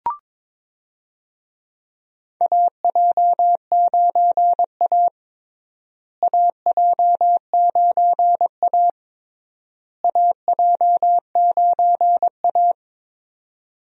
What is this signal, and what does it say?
Morse code, heard as a keyed tone, about 22 wpm, 700 Hz. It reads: AJ9A AJ9A AJ9A
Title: USA Call Signs - 4 Characters - Repeated 3x - 22wpm